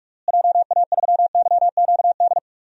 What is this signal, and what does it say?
Morse code, heard as a keyed tone, about 46 wpm, 700 Hz. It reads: JA3YQD
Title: DXCC-100 Call Signs - 6 Characters - Repeated 3x - 45wpm